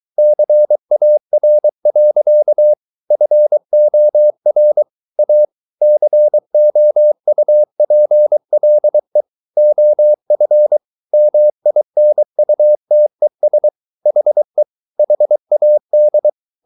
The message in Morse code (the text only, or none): CAR. FOR A COUPLE OF MINUTES HE HAD